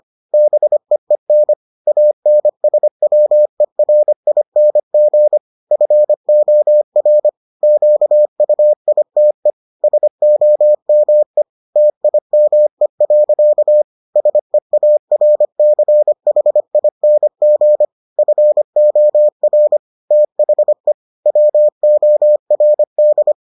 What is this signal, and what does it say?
Morse code, heard as a keyed tone, about 25 wpm, 600 Hz. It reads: BEEN ANSWERING FOR QUITE SOME TIME. SEARCHING FOR THE WORD